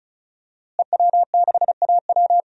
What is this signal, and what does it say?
Morse code, heard as a keyed tone, about 36 wpm, 700 Hz. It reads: EW6AW
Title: DXCC-100 Call Signs - 5 Characters - Repeated 2x - 35wpm